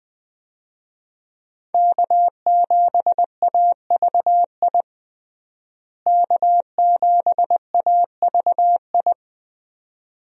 Morse code, heard as a keyed tone, about 20 wpm, 700 Hz. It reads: K7AVI K7AVI
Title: USA Call Signs - 5 Characters - Repeated 2x - 20wpm